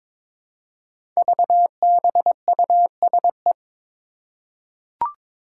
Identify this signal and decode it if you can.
Morse code, heard as a keyed tone, about 22 wpm, 700 Hz. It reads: VBUSE
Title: Five Challenging Letter-Number Combinations 22wpm